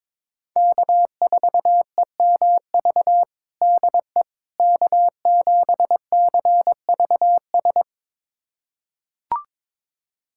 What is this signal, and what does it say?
Morse code, heard as a keyed tone, about 21 wpm, 700 Hz. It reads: K4EMV DE K7CVS
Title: USA Callsign DE Callsign - 5 Characters - 22wpm